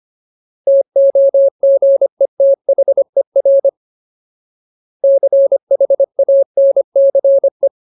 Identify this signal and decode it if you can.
Morse code, heard as a keyed tone, about 25 wpm, 550 Hz. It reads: TOGETHER CHANCE
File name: Ultimate Word List / 25WPM Ultimate